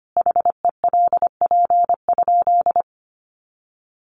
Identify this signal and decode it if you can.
Morse code, heard as a keyed tone, about 25 wpm, 700 Hz. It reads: HELP?
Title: Sentences from Top 1400 Words 25wpm